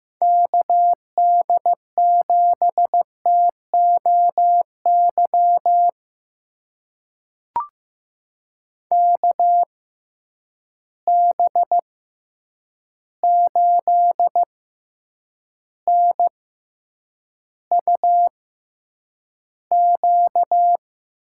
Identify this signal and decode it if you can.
Morse code, heard as a keyed tone, about 15 wpm, spaced as for 8 wpm, 700 Hz. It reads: KD7TOY KB8NUQ
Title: USA Call Signs - 6 Characters - 6x Character Spacing - 15wpm